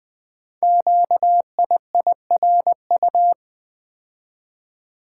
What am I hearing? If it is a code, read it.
Morse code, heard as a keyed tone, about 20 wpm, 700 Hz. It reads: QIIRU
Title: Code Groups - 5 Letters - 20wpm